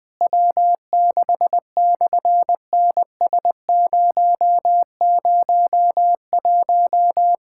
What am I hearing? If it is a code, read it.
Morse code, heard as a keyed tone, about 18 wpm, 700 Hz. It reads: W6/NS001
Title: SOTA QSO Practice 20wpm